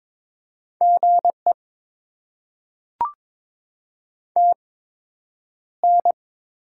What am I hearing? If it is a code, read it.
Morse code, heard as a keyed tone, about 22 wpm, spaced as for 9 wpm, 700 Hz. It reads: GE TN
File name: Top 20 QSO Elements - 8x Character Spacing - 22wpm